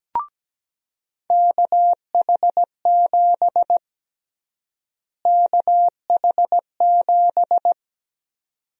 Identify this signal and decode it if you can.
Morse code, heard as a keyed tone, about 17 wpm, 700 Hz. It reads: KH7 KH7
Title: USA Call Sign Prefixes - Repeated 2x - 17wpm